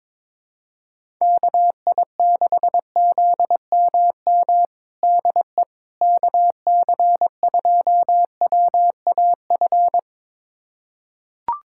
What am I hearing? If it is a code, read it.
Morse code, heard as a keyed tone, about 22 wpm, 700 Hz. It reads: KI6ZMM DE KC2WAF
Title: USA Callsign DE Callsign 22wpm